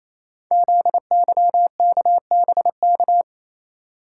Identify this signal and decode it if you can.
Morse code, heard as a keyed tone, about 29 wpm, 700 Hz. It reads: ZYKBK
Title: Code Groups - 5 Letters - 28wpm